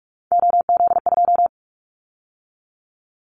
Morse code, heard as a keyed tone, about 46 wpm, 700 Hz. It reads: OZ2
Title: Callsign Prefixes for Top 50 DXCC - 45wpm